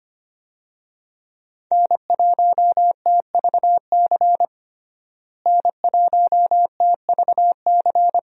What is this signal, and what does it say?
Morse code, heard as a keyed tone, about 25 wpm, 700 Hz. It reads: N1TVC N1TVC
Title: USA Call Signs - 5 Characters - Repeated 2x - 25wpm